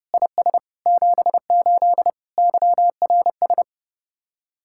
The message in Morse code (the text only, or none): IS 78 YRS